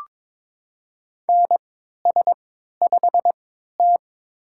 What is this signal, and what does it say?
Morse code, heard as a keyed tone, about 23 wpm, spaced as for 13 wpm, 700 Hz. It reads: NS5T
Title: USA Call Signs - 4 Characters - 3x Character Spacing - 22wpm